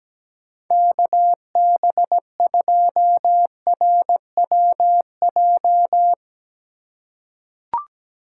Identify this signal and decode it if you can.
Morse code, heard as a keyed tone, about 17 wpm, 700 Hz. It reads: KB2RWJ
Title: USA Call Signs - 6 Characters - 2x Character Spacing - 17wpm